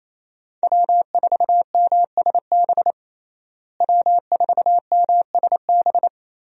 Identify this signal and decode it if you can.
Morse code, heard as a keyed tone, about 28 wpm, 700 Hz. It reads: W4MSB W4MSB
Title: USA Call Signs - 5 Characters - Repeated 2x - 28wpm